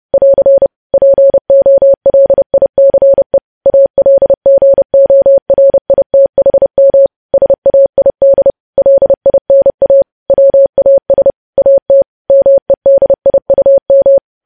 Morse code, heard as a keyed tone, about 30 wpm, 550 Hz. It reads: <AR> POLICE ALGORITHM SAID LINA WAS AT MEDIUM
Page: news headlines in morse code at 30 wpm